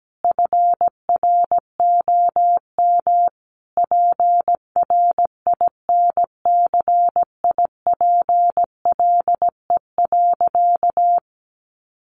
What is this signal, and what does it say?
Morse code, heard as a keyed tone, about 17 wpm, 700 Hz. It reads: FROM PRINCIPLE.
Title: Sentences from Top 1400 Words 17wpm